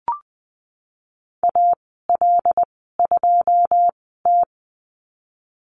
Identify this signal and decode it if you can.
Morse code, heard as a keyed tone, about 20 wpm, 700 Hz. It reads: A L 2 T
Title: 082 Lesson 14g - Number 2 - Call Signs 20wpm